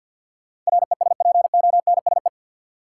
Morse code, heard as a keyed tone, about 52 wpm, 700 Hz. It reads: RESPONSE
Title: Sentences from Top 1400 Words - New Word Drill - 50wpm